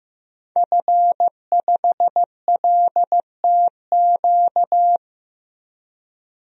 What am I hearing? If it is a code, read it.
Morse code, heard as a keyed tone, about 15 wpm, 700 Hz. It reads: F5LTQ
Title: DXCC-100 Call Signs - 5 Characters - 15wpm